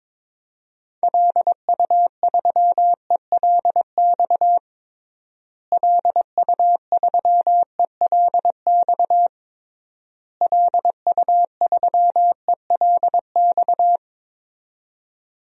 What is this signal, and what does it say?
Morse code, heard as a keyed tone, about 22 wpm, 700 Hz. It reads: LU3ELX LU3ELX LU3ELX
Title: DXCC-100 Call Signs - 6 Characters - Repeated 3x - 22wpm